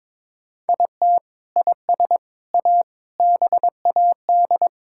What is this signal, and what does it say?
Morse code, heard as a keyed tone, about 22 wpm, 700 Hz. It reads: IT IS A BAD
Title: Sentences from Top 1000 Words - Review Part 6 - 22wpm